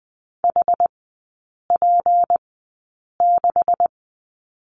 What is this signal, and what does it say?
Morse code, heard as a keyed tone, about 20 wpm, spaced as for 8 wpm, 700 Hz. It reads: HP6
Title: 191 Lesson 31e - Letter v - Three Characters 20wpm